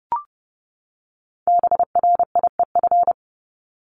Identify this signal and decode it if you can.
Morse code, heard as a keyed tone, about 30 wpm, 700 Hz. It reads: BRIEF
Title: LY Suffix - Top 100 Words with Base Words 30wpm